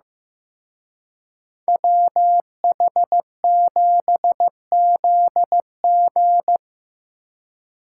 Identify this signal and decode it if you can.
Morse code, heard as a keyed tone, about 18 wpm, 700 Hz. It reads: WH7ZG
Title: Speed Racing - USA 2x2 Call Signs 15wpm